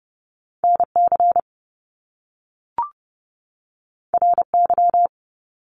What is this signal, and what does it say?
Morse code, heard as a keyed tone, about 30 wpm, 700 Hz. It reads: NC RY
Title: Top Two Letter Combinations in Words 30wpm